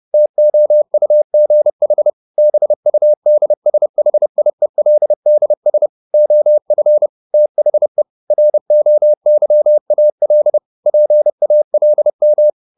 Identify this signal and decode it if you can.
Morse code, heard as a keyed tone, about 30 wpm, 600 Hz. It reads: TOUGH BUDSHIELDS OF THE ROYAL PALM
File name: the old man and the sea / the old man and the sea 1 30wpm